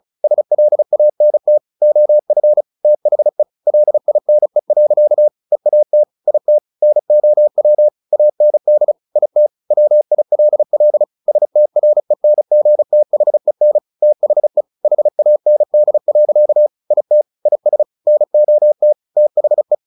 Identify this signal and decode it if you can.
Morse code, heard as a keyed tone, about 34 wpm, 600 Hz. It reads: SLANT OF THE LINE. EAT IT NOW AND IT WILL STRENGTHEN THE HAND. IT IS NOT THE
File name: the old man and the sea / the old man and the sea 1 35wpm